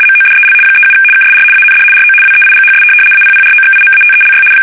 RTTY, about 75 baud. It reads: MODEL 37 KSR AND RO WEIGHS APPROXIMATELY
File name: fsk850-75B / teletype-corp